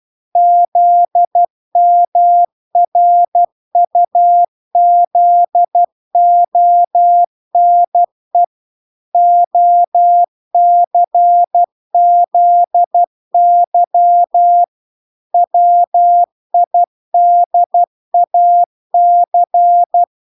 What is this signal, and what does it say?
Morse code, heard as a keyed tone, about 12 wpm, 700 Hz. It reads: ZMRUZONE OCZY WIDAC